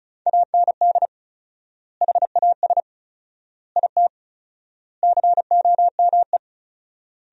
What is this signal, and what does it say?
Morse code, heard as a keyed tone, about 34 wpm, 700 Hz. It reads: AND HAS IT COME
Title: 4x Word-Spacing - Sentences from Top 100 Words - 35wpm